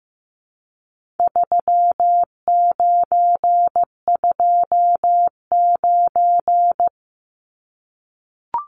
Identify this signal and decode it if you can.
Morse code, heard as a keyed tone, about 15 wpm, 700 Hz. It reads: 3929
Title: Numbers - 4-Digit 15wpm